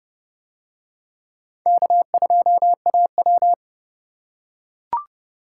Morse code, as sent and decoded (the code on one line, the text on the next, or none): -.- ..--- .- .--
K2AW